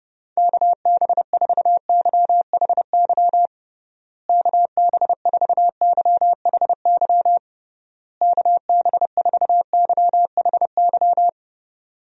Morse code, heard as a keyed tone, about 30 wpm, 700 Hz. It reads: KB4YHY KB4YHY KB4YHY
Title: USA Call Signs - 6 Characters - Repeated 3x - 30wpm